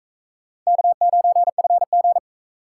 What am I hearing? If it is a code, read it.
Morse code, heard as a keyed tone, about 43 wpm, 700 Hz. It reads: K9FG